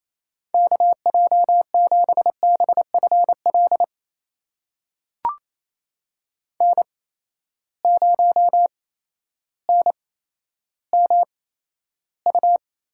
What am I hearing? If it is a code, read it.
Morse code, heard as a keyed tone, about 28 wpm, spaced as for 13 wpm, 700 Hz. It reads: KJ7BFL N0NMU
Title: USA Call Signs - 8x Character Spacing - 28wpm